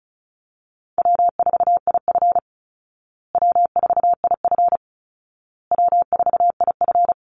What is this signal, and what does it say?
Morse code, heard as a keyed tone, about 34 wpm, 700 Hz. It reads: W4IF W4IF W4IF
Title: Call Signs Repeated 2x - 35wpm